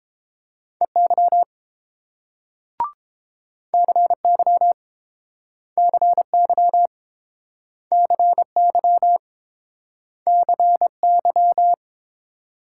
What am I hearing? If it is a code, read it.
Morse code, heard as a keyed tone, about 31 wpm, 700 Hz. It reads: EY CY CY CY CY